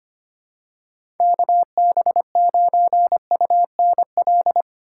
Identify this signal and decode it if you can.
Morse code, heard as a keyed tone, about 25 wpm, 700 Hz. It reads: KB9UNL
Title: USA Call Signs - 4x Character Spacing - 25wpm